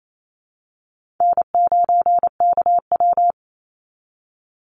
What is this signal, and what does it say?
Morse code, heard as a keyed tone, about 28 wpm, 700 Hz. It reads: N9KW
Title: NA SOTA Chaser Callsigns 28wpm